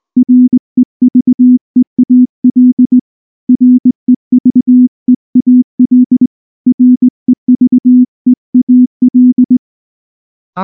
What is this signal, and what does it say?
Morse code, heard as a keyed tone, about 18 wpm, 250 Hz. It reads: REVEAL REVEAL REVEAL
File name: morse-training-20WPM